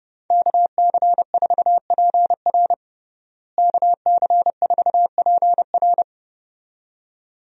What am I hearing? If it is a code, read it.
Morse code, heard as a keyed tone, about 30 wpm, 700 Hz. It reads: KC4PR KC4PR
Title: USA Call Signs - 5 Characters - Repeated 2x - 30wpm